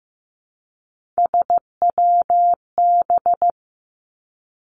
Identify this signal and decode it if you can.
Morse code, heard as a keyed tone, about 15 wpm, 700 Hz. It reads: SWB
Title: Three Challenging Letter-Number Combinations 15wpm